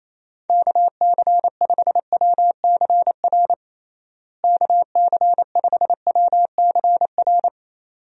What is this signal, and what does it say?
Morse code, heard as a keyed tone, about 28 wpm, 700 Hz. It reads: KC5WCR KC5WCR
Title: USA Call Signs - 6 Characters - Repeated 2x - 28wpm